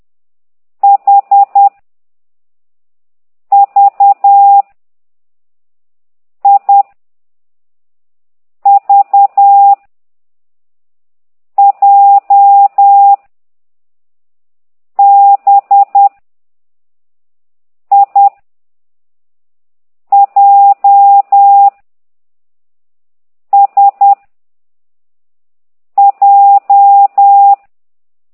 Morse code, heard as a keyed tone, about 10 wpm, 800 Hz. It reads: H V I V J B I J S J
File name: ct2idw / exerc6